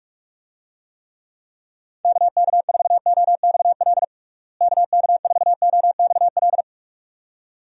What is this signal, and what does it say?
Morse code, heard as a keyed tone, about 43 wpm, 700 Hz. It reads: KK4OXL KK4OXL
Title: USA Call Signs - 6 Characters - Repeated 2x - 45wpm